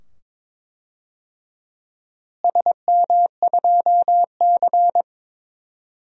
Morse code, heard as a keyed tone, about 21 wpm, 700 Hz. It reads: SM2C
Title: DXCC-100 Call Signs - 4 Characters - Repeated 2x - 22wpm